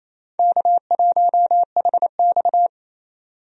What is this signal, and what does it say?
Morse code, heard as a keyed tone, about 28 wpm, 700 Hz. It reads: K1HX
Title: USA Call Signs - 4 Characters - 28wpm